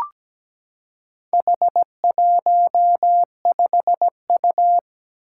Morse code, heard as a keyed tone, about 17 wpm, 700 Hz. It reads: H15U
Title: Four Challenging Letter-Number Combinations 17wpm